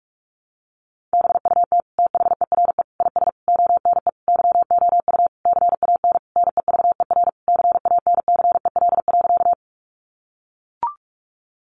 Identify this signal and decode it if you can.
Morse code, heard as a keyed tone, about 44 wpm, 700 Hz. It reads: BUT THERE IS ONE YOU CAN NEVER CANCEL.
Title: Sentences from Top 1200 Words 45wpm